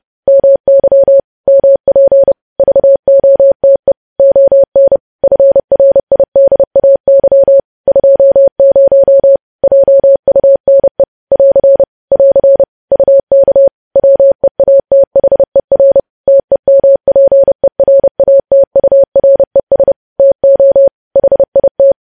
Morse code, heard as a keyed tone, about 30 wpm, 550 Hz. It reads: MY MP VOTE ON FRIDAY 20 JUNE <AR> <AR> UK WEATHER TEMPERATURES TO HIT
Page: news headlines in morse code at 30 wpm